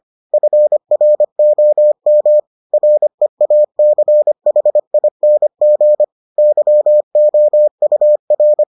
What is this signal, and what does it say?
Morse code, heard as a keyed tone, about 25 wpm, 600 Hz. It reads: FROM REACHING YOUR